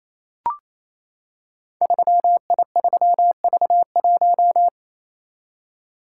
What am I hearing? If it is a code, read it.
Morse code, heard as a keyed tone, about 28 wpm, 700 Hz. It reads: 3I3V1
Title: Code Groups - 5 Letter:Numbers - 28wpm